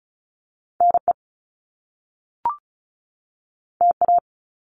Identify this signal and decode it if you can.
Morse code, heard as a keyed tone, about 36 wpm, 700 Hz. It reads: NE TA
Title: Top Two Letter Combinations in Words 35wpm